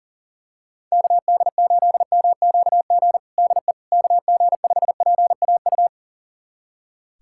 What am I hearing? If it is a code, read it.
Morse code, heard as a keyed tone, about 40 wpm, 700 Hz. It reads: KD8MQG DE KG5PAU
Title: USA Callsign DE Callsign 40wpm